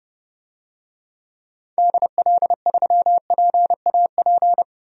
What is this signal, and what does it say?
Morse code, heard as a keyed tone, about 32 wpm, 700 Hz. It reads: DL3PAP
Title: DXCC-100 Call Signs - 6 Characters - Repeated 2x - 30wpm